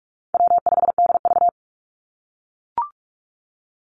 Morse code, heard as a keyed tone, about 44 wpm, 700 Hz. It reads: W5DV
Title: Easy Call Signs - 45wpm